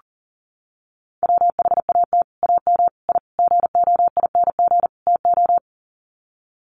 Morse code, heard as a keyed tone, about 37 wpm, 700 Hz. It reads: WHAT AM I GOING TO
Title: Sets of 5 Words Encore 40wpm